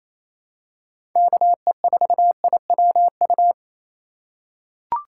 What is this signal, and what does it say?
Morse code, heard as a keyed tone, about 28 wpm, 700 Hz. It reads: KE4IWU
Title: USA Call Signs - 3x Character Spacing - 28wpm